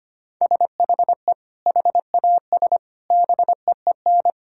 Morse code, heard as a keyed tone, about 25 wpm, 700 Hz. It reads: SHE HAS BEEN
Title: Sentences from Top 1000 Words - Review Part 6 - 25wpm